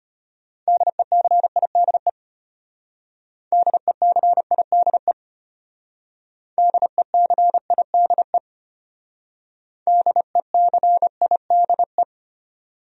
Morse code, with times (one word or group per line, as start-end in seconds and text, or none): 0.7-2.1 DECIDE
3.5-5.1 DECIDE
6.6-8.4 DECIDE
9.9-12.0 DECIDE